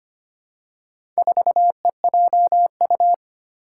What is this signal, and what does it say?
Morse code, heard as a keyed tone, about 25 wpm, 700 Hz. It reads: 4EJU
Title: Four Challenging Letter-Number Combinations 25wpm